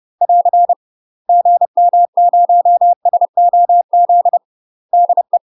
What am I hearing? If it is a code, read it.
Morse code, heard as a keyed tone, about 30 wpm, 700 Hz. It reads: <AR> GM0SOZ DE
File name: UK MASTER QSOs TRAINING / 30WPM / UK MASTER QSOs 0033